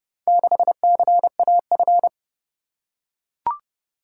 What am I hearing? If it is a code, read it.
Morse code, heard as a keyed tone, about 27 wpm, 700 Hz. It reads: 6CAF